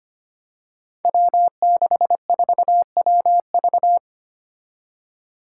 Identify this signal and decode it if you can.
Morse code, heard as a keyed tone, about 25 wpm, 700 Hz. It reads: W64WV
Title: Five Challenging Letter-Number Combinations 25wpm